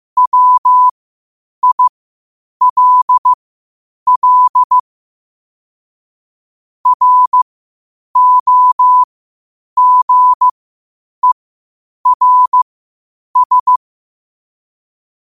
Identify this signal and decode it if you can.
Morse code, heard as a keyed tone, about 15 wpm, spaced as for 8 wpm, 1000 Hz. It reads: WILL ROGERS
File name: qotdfiles / 8wpm